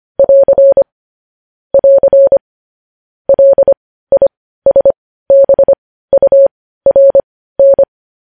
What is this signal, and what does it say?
Morse code, heard as a keyed tone, about 25 wpm, spaced as for 15 wpm, 550 Hz. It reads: <AR> <AR> LISBURN
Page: News headlines in morse code at 15 wpm